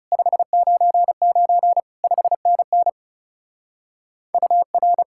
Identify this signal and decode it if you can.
Morse code, heard as a keyed tone, about 30 wpm, 700 Hz. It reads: 599 5NN UR
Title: Speed-Racing Common QSO Phrases - 30wpm